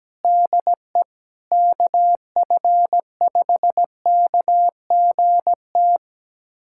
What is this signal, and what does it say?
Morse code, heard as a keyed tone, about 17 wpm, 700 Hz. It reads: DE KF5KGT
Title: USA Callsign DE Callsign - 6 Characters - 17wpm